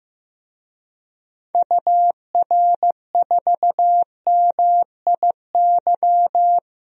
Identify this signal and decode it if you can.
Morse code, heard as a keyed tone, about 15 wpm, 700 Hz. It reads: UR4MIY
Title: DXCC-100 Call Signs - 6 Characters - 15wpm